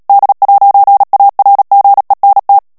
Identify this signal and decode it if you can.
Morse code, heard as a keyed tone, about 37 wpm, 800 Hz. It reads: D'ARGENT
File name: CW / FICHIERS-MP3 / FRANCAIS TEXTE CLAIR 1 VITESSE 35 MOTS MINUTES